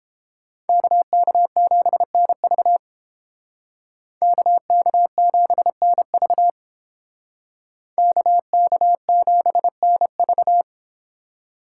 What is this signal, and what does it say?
Morse code, heard as a keyed tone, about 30 wpm, 700 Hz. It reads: KK7NV KK7NV KK7NV